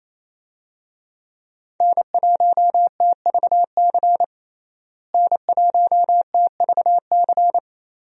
Morse code, heard as a keyed tone, about 28 wpm, 700 Hz. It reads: N1TVC N1TVC
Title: USA Call Signs - 5 Characters - Repeated 2x - 28wpm